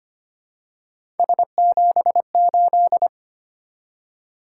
Morse code, heard as a keyed tone, about 25 wpm, 700 Hz. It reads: S78
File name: Three Letter-Number Combinations - 25wpm